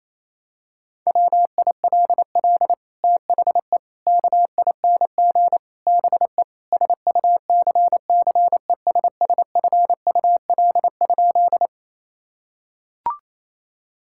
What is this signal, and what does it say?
Morse code, heard as a keyed tone, about 28 wpm, 700 Hz. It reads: WILL THE KING BE SUCCESSFUL?
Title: Sentences from Top 1000 Words - Review Part 6 - 28wpm